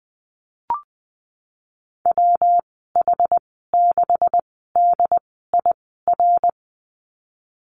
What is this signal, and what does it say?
Morse code, heard as a keyed tone, about 19 wpm, spaced as for 14 wpm, 700 Hz. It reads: WH6DIR
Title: 161 Lesson 26g - Letter g - Call Signs 20wpm